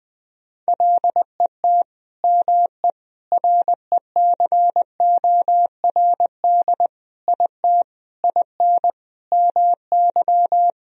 Morse code, heard as a keyed tone, about 21 wpm, 700 Hz. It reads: LET ME RECORD IT IN MY
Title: Sentences from Top 1000 Words - Review Part 3 - 20wpm